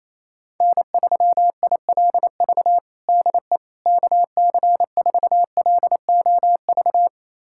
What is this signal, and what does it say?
Morse code, heard as a keyed tone, about 28 wpm, 700 Hz. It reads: N3ILV DE KC4LOV
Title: USA Callsign DE Callsign 28wpm